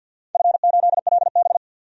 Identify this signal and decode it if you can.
Morse code, heard as a keyed tone, about 47 wpm, 700 Hz. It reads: F8LB
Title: DXCC-100 Call Signs - 4 Characters - 50wpm